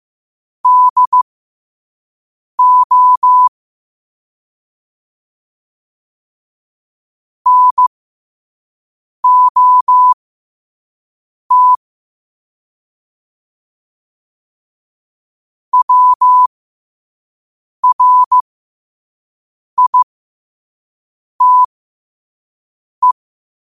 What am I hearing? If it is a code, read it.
Morse code, heard as a keyed tone, about 15 wpm, spaced as for 5 wpm, 1000 Hz. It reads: DO NOT WRITE
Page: 5 WPM morse code quotes